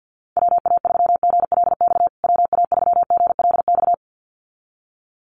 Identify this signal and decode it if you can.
Morse code, heard as a keyed tone, about 50 wpm, 700 Hz. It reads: WA3GLX WA3GLX
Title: USA Call Signs Repeated 2x 50wpm